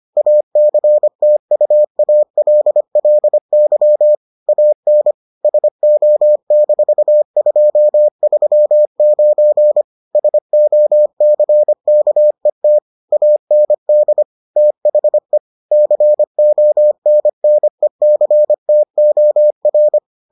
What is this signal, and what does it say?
Morse code, heard as a keyed tone, about 25 wpm, 600 Hz. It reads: ACTUALLY AN SO-239 SOCKET AND THE CONNECTOR